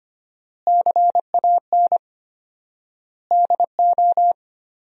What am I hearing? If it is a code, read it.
Morse code, heard as a keyed tone, about 25 wpm, 700 Hz. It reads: CAN DO
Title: 4x Word-Spacing - Sentences from Top 200 Words - 25wpm